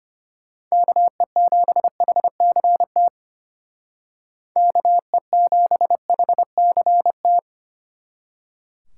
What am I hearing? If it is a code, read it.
Morse code, heard as a keyed tone, about 30 wpm, 700 Hz. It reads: KE7HCT KE7HCT
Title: Speed Racing - USA 2x3 Call Signs 25wpm